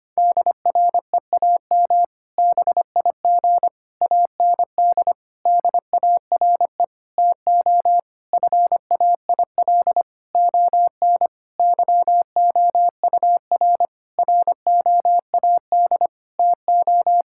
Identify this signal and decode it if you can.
Morse code, heard as a keyed tone, about 25 wpm, 700 Hz. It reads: DREAM BIG AND DARE TO FAIL ON YOUR ROAD TO
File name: practice / 25wpm / Difficult Sentences with Hints - 25wpm